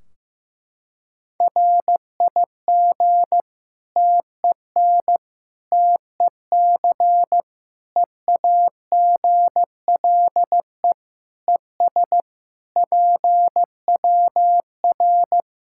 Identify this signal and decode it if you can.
Morse code, heard as a keyed tone, about 15 wpm, 700 Hz. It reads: RIG TEN TEC EAGLE ES PWR
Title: Practice QSO - Exchange 2 - Info - 15wpm